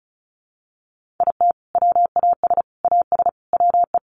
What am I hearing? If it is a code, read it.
Morse code, heard as a keyed tone, about 34 wpm, 700 Hz. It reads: IT WAS AS WE
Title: Sentences from Top 1500 Words 35wpm